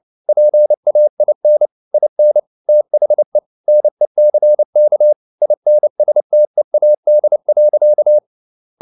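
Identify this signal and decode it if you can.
Morse code, heard as a keyed tone, about 29 wpm, 600 Hz. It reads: PAIN IN THE NECK INSTEAD.
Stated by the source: Morse code at 29 wpm, sending Ken Follett's novel Never in English, from CW med Gnister.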